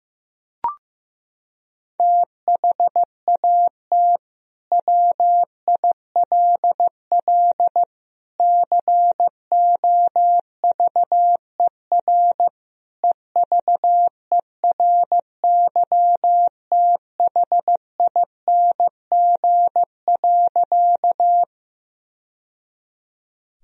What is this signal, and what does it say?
Morse code, heard as a keyed tone, about 16 wpm, 700 Hz. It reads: THAT WILL COVER EVERYTHING.
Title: Sentences from Top 1000 Words - Review Part 1 - 15wpm